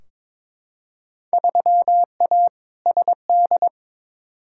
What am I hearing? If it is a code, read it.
Morse code, heard as a keyed tone, about 22 wpm, 700 Hz. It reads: 3A SD
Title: Field Day Exchanges - 22wpm